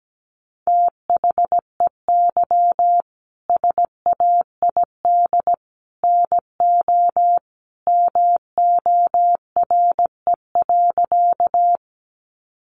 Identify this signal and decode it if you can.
Morse code, heard as a keyed tone, about 17 wpm, 700 Hz. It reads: THEY SAID NO MORE.
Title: Sentences from Top 100 Words 17wpm